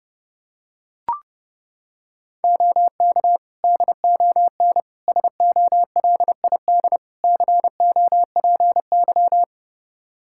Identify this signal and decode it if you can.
Morse code, heard as a keyed tone, about 30 wpm, 700 Hz. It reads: OK DON SOLID COPY
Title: Practice QSO - Exchange 3 - More Info - 30wpm